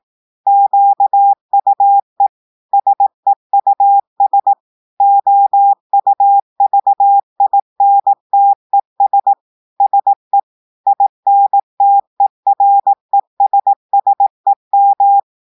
Morse code, heard as a keyed tone, about 18 wpm, 800 Hz. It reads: QUE SEUS OUVINTES SE INTERESSEM
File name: audios / portugais / portugais texte en clair 2 vitesse 18 mots minutes